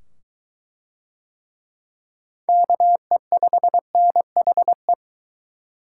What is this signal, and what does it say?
Morse code, heard as a keyed tone, about 20 wpm, 700 Hz. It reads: KE5NHE